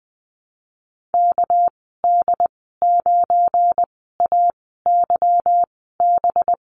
Morse code, heard as a keyed tone, about 20 wpm, spaced as for 15 wpm, 700 Hz. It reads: KD9AYB
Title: 213 Lesson 34g - Number 8 - Call Signs 20wpm